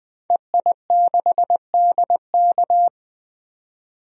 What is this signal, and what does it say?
Morse code, heard as a keyed tone, about 20 wpm, 700 Hz. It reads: EI6DK
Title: DXCC-100 Call Signs - 5 Characters - 20wpm